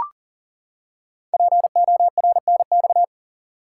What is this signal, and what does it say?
Morse code, heard as a keyed tone, about 39 wpm, 690 Hz. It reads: PORNX
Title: Code Groups - 5 Letters - 40wpm